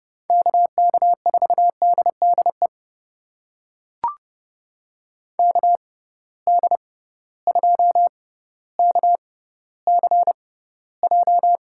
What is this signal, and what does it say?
Morse code, heard as a keyed tone, about 30 wpm, spaced as for 15 wpm, 700 Hz. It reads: KK4DDE KD2KCJ
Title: USA Call Signs - 6 Characters - 6x Character Spacing - 30wpm